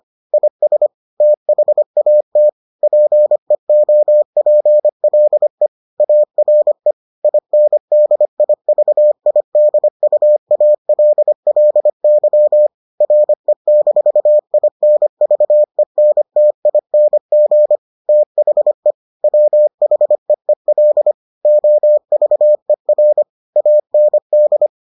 Morse code, heard as a keyed tone, about 25 wpm, 600 Hz. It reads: IS THAT PEOPLE ARE INDIVIDUALLY RE-INVENTING THE WHEEL OVER AND